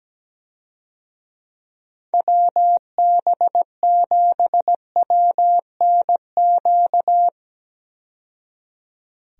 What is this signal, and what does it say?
Morse code, heard as a keyed tone, about 17 wpm, 700 Hz. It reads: WB7WNQ
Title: USA Call Signs - 6 Characters - 17wpm